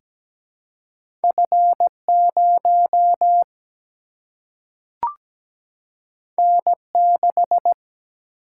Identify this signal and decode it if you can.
Morse code, heard as a keyed tone, about 17 wpm, 700 Hz. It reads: F0 N6
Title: Single-Letter Single-Number 17wpm